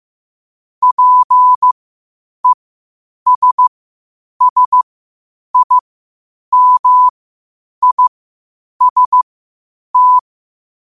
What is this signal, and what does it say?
Morse code, heard as a keyed tone, about 15 wpm, spaced as for 8 wpm, 1000 Hz. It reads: PESSIMIST
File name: qotdfiles / 8wpm